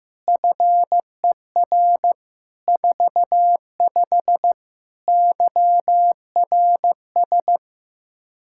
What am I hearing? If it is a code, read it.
Morse code, heard as a keyed tone, about 15 wpm, 700 Hz. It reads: FER 45 YRS
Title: Practice QSO - Exchange 3 - More Info - 15wpm